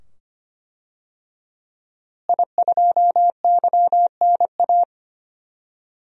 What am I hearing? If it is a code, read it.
Morse code, heard as a keyed tone, about 25 wpm, 700 Hz. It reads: I2YNA
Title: DXCC-100 Call Signs - 5 Characters - Repeated 2x - 25wpm